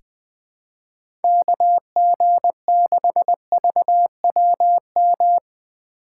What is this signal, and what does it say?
Morse code, heard as a keyed tone, about 20 wpm, 700 Hz. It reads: KG6VWM
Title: USA Call Signs - 6 Characters - 2x Character Spacing - 20wpm